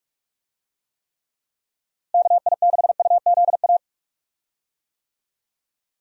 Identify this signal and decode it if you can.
Morse code, heard as a keyed tone, about 46 wpm, 700 Hz. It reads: KI6UZA
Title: USA Call Signs - 6 Characters - Repeated 2x - 45wpm